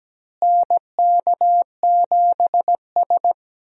Morse code, heard as a keyed tone, about 17 wpm, 700 Hz. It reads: NK7S
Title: USA Call Signs - 4 Characters - Repeated 3x - 17wpm